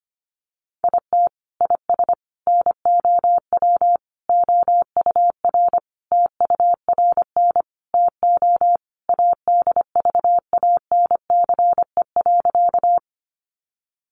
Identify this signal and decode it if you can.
Morse code, heard as a keyed tone, about 25 wpm, 700 Hz. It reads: IT IS NOW OUR TURN TO ADVANCE.
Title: Sentences from Top 1400 Words 25wpm